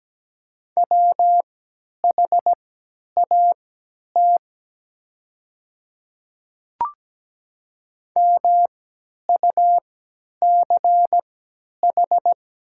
Morse code, heard as a keyed tone, about 17 wpm, spaced as for 10 wpm, 700 Hz. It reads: WHAT MUCH
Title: 3x Character Spacing - Top 200 Words - 17wpm